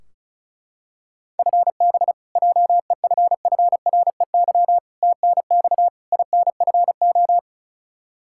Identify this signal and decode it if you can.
Morse code, heard as a keyed tone, about 36 wpm, 700 Hz. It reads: FB JEFFREY TNX INFO
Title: Practice QSO - Exchange 3 - More Info - 35wpm